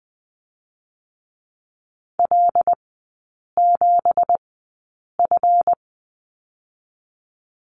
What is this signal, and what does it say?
Morse code, heard as a keyed tone, about 20 wpm, 700 Hz. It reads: L 7 F
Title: 218 Lesson 35e - Number 0 - Three Characters 20wpm